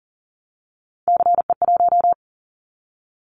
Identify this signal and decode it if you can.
Morse code, heard as a keyed tone, about 40 wpm, 700 Hz. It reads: CE1
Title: Callsign Prefixes for Top 100 DXCC - 40wpm